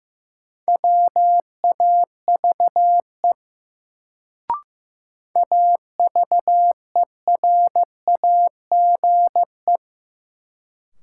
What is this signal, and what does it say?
Morse code, heard as a keyed tone, about 15 wpm, 700 Hz. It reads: WAVE AVERAGE
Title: Sentences from Top 1400 Words - New Word Drill Alt - 15wpm